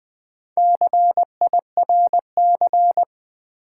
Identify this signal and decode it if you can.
Morse code, heard as a keyed tone, about 21 wpm, 700 Hz. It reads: CIRC
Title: Build A Word - 4 Letters At A Time - 11-Letter Words - 20wpm